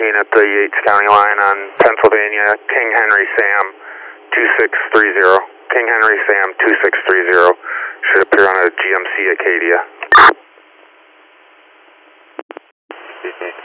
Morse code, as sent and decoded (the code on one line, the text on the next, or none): none